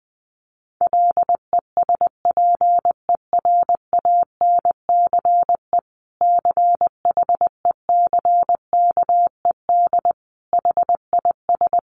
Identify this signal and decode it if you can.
Morse code, heard as a keyed tone, about 20 wpm, 700 Hz. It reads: LESPERANCE CHECKED HIS
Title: A Sound Of Thunder 20wpm